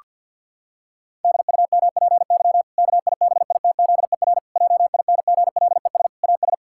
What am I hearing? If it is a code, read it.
Morse code, heard as a keyed tone, about 48 wpm, 700 Hz. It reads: DUMPY KIBITZER JINGLES AS
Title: Pangram Challenge v2 - 50wpm